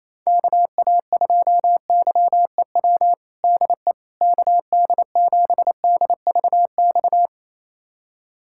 Morse code, heard as a keyed tone, about 28 wpm, 700 Hz. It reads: KA2YEW DE KD7DVX
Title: USA Callsign DE Callsign - 6 Characters - 28wpm